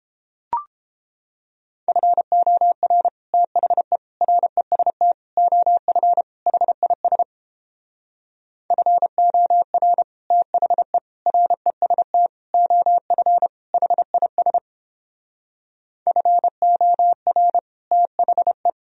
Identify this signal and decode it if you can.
Morse code, heard as a keyed tone, about 31 wpm, 700 Hz. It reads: FOR THE REST OF HIS FOR THE REST OF HIS FOR THE